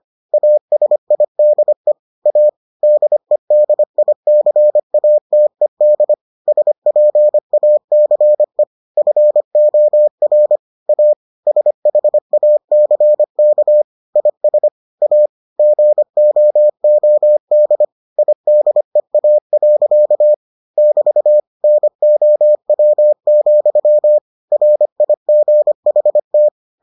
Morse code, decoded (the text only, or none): ASIDE A DEDICATED SPACE FOR A SHACK IS A GOOD IDEA. <BT> NOW, RIGHT